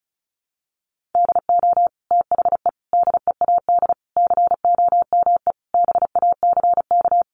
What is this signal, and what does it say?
Morse code, heard as a keyed tone, about 34 wpm, 700 Hz. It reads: DO THE DEAD COME BACK
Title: Sentences from Top 1500 Words 35wpm